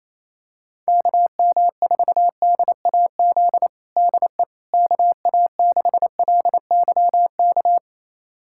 Morse code, heard as a keyed tone, about 28 wpm, 700 Hz. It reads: KM4DAZ DE KA6LYK
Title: USA Callsign DE Callsign 28wpm